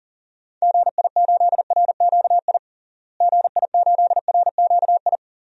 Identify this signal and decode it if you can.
Morse code, heard as a keyed tone, about 40 wpm, 700 Hz. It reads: GI8RQI GI8RQI
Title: DXCC-100 Call Signs - 6 Characters - Repeated 2x - 40wpm